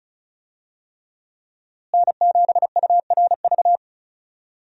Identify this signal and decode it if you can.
Morse code, heard as a keyed tone, about 35 wpm, 700 Hz. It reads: N7URV
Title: USA Call Signs - 5 Characters - Repeated 3x - 35wpm